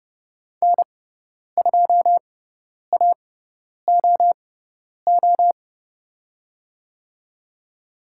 Morse code, heard as a keyed tone, about 30 wpm, spaced as for 10 wpm, 700 Hz. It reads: N2AOO
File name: Lesson 16g - Letter c - Call Signs 30wpm